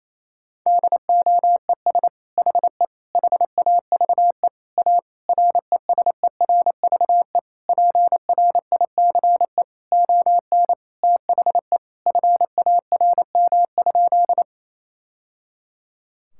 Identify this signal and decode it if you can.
Morse code, heard as a keyed tone, about 28 wpm, 700 Hz. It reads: DOES HE HAVE A RESERVE PRICE ON THE FARM?
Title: Sentences from Top 1000 Words - Review Part 6 - 28wpm